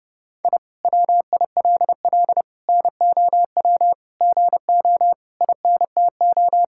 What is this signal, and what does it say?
Morse code, heard as a keyed tone, about 30 wpm, 700 Hz. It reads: I WILL NOW GO INTO
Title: Sentences from Top 1000 Words - Review Part 6 - 30wpm